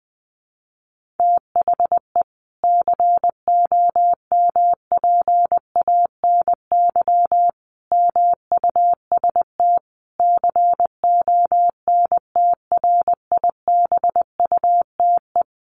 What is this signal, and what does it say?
Morse code, heard as a keyed tone, about 20 wpm, 700 Hz. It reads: THE COMPANY MUST CONTRIBUTE
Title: Sentences from Top 1400 Words 20wpm